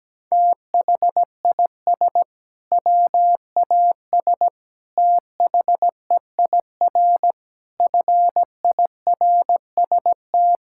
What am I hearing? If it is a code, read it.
Morse code, heard as a keyed tone, about 17 wpm, 700 Hz. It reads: THIS WAS THEIR FIRST
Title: Sentences from Top 1600 Words 17wpm